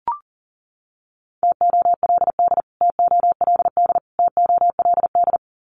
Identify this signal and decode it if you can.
Morse code, heard as a keyed tone, about 40 wpm, 700 Hz. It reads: TOLD TOLD TOLD
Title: Top 500 Words Repeated 40wpm